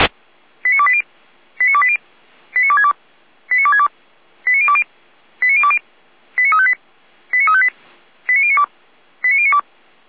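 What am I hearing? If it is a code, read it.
Five-tone selective call (ZVEI), heard as quick runs of five tones, tones of about 70 ms. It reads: 89280 89280 89262 89262 89020 89020 89358 89358 89092 89092
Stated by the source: ZVEI five-tone selective call received by SDR.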